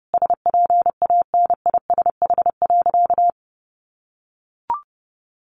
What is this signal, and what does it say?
Morse code, heard as a keyed tone, about 28 wpm, 700 Hz. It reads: SPANISH.
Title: Additional Easy Phrases 30wpm